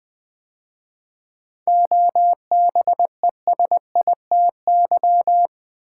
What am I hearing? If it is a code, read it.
Morse code, heard as a keyed tone, about 20 wpm, 700 Hz. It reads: OBESITY
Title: 7 Letter Words - First 3 Letters Given - 20wpm